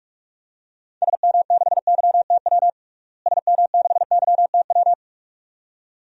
Morse code, heard as a keyed tone, about 41 wpm, 700 Hz. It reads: SM6YTW SM6YTW
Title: DXCC-100 Call Signs - 6 Characters - Repeated 2x - 45wpm